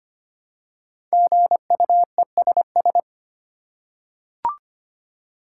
Morse code, heard as a keyed tone, about 25 wpm, 700 Hz. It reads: GUESS
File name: 5 Letter Words - First 3 Letters Given - 25wpm